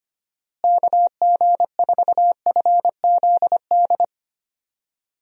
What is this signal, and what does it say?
Morse code, heard as a keyed tone, about 25 wpm, 700 Hz. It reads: KG4FZD
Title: USA Call Signs - 6 Characters - Repeated 2x - 25wpm